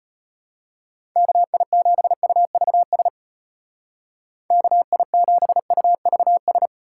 Morse code, heard as a keyed tone, about 32 wpm, 700 Hz. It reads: KI7UVS KI7UVS